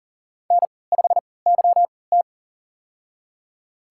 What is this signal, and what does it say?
Morse code, heard as a keyed tone, about 37 wpm, spaced as for 22 wpm, 700 Hz. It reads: N5YT
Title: USA Call Signs - 4 Characters - 3x Character Spacing - 40wpm